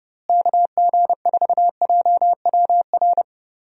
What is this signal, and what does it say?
Morse code, heard as a keyed tone, about 29 wpm, 700 Hz. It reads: KG4JWR
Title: USA Call Signs - 6 Characters - Repeated 2x - 30wpm